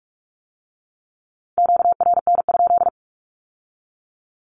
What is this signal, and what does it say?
Morse code, heard as a keyed tone, about 46 wpm, 700 Hz. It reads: QRN?
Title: Q-Codes with Spoken Meaning - 45wpm